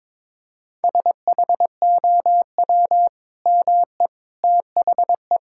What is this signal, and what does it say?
Morse code, heard as a keyed tone, about 22 wpm, 700 Hz. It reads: SHOW ME THE
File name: practice / 22wpm / Sentences with 2-Word Phrases - 22wpm